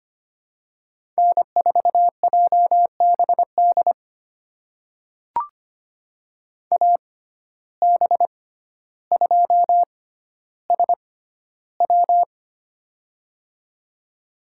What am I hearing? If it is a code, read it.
Morse code, heard as a keyed tone, about 25 wpm, spaced as for 13 wpm, 700 Hz. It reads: N4JBD AB2SW
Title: USA Call Signs - 5 Characters - 6x Character Spacing - 25wpm